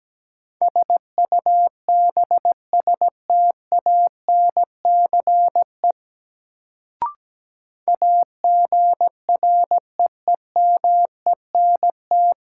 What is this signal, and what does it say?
Morse code, heard as a keyed tone, about 17 wpm, 700 Hz. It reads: SUBSTANCE AGREEMENT
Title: Sentences from Top 1500 Words - New Word Drill Alt - 17wpm